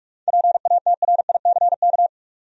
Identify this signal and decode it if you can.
Morse code, heard as a keyed tone, about 46 wpm, 700 Hz. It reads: PATRICK
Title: USA Common Names - 45wpm